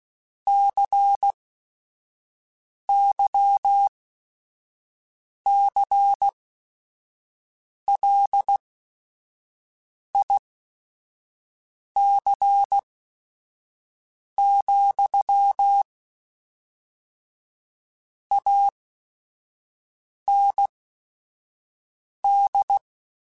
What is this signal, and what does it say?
Morse code, heard as a keyed tone, about 16 wpm, spaced as for 5 wpm, 800 Hz. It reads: CYCLIC, AND
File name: Technical Subjects / Technical Subjects-mp3s / XC @5